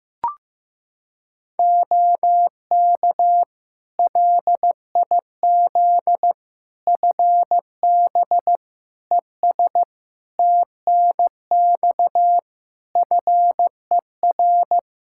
Morse code, heard as a keyed tone, about 16 wpm, 700 Hz. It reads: OK LIZ FB ES TNX FER
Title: K1USN SST QSO Practice 15wpm